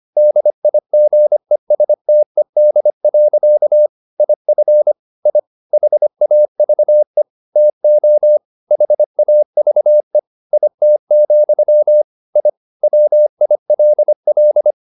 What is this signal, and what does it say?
Morse code, heard as a keyed tone, about 25 wpm, 600 Hz. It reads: DIGESTED. IF I HAVE TO HAVE IT, I WILL
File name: the old man and the sea part 1 25wpm